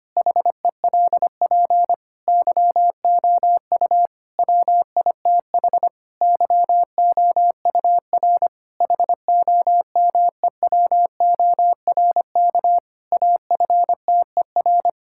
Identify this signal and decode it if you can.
Morse code, heard as a keyed tone, about 25 wpm, 700 Hz. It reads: HELP YOU WITH YOUR HOMEWORK AFTER
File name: practice / 25wpm / Difficult Sentences with Hints - 25wpm